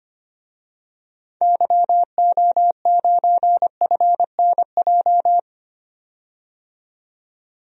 Morse code, heard as a keyed tone, about 25 wpm, 700 Hz. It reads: YO9FNJ
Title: DXCC-100 Call Signs - 6 Characters - 25wpm